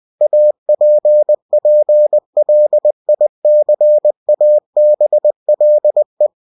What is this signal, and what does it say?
Morse code, heard as a keyed tone, about 20 wpm, 600 Hz. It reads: APPLICABLE
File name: cw-books / winnie-the-pooh / winnie-20wpm-page-0117